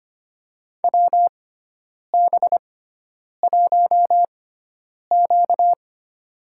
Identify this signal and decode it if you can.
Morse code, heard as a keyed tone, about 25 wpm, spaced as for 9 wpm, 700 Hz. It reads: WB1Q
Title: USA Call Signs - 6x Character Spacing - 25wpm